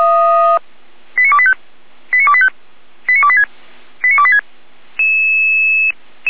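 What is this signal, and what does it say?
Five-tone selective call (ZVEI), heard as quick runs of five tones, tones of about 70 ms. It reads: 89285 89285 89286 89286
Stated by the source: SDR-received ZVEI five-tone call, code 89270, archived by feuerwehr-orlamuende (fire brigade Orlamünde) server f85360.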